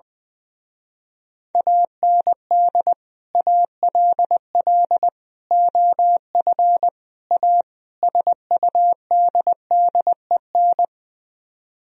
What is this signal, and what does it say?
Morse code, heard as a keyed tone, about 21 wpm, 700 Hz. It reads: AND ALL OF A SUDDEN
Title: Speed-Racing - Sets of 5 Words 20wpm